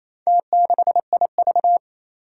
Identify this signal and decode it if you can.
Morse code, heard as a keyed tone, about 28 wpm, 700 Hz. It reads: T6IV
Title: Four Challenging Letter-Number Combinations 28wpm